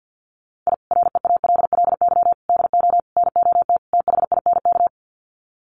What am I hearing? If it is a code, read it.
Morse code, heard as a keyed tone, about 50 wpm, 700 Hz. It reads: I REALLY DO NOT THINK
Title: Sets of 5 Words Encore2 50wpm